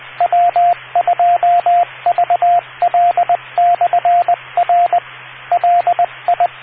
Morse code, heard as a keyed tone, about 20 wpm, 700 Hz, with radio noise. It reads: W2VL/R LI